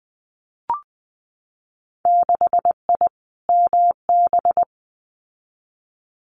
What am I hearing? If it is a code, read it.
Morse code, heard as a keyed tone, about 20 wpm, 700 Hz. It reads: 6I MB
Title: Winter Field Day Exchanges Encore 20wpm